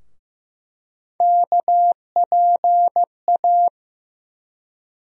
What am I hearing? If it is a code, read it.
Morse code, heard as a keyed tone, about 15 wpm, 700 Hz. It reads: KPA
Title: USA Call Sign Postfixes - 3 Letters 15wpm